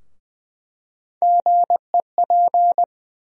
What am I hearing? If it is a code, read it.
Morse code, heard as a keyed tone, about 21 wpm, 700 Hz. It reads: GEP
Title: USA Call Sign Postfixes - 3 Letters 20wpm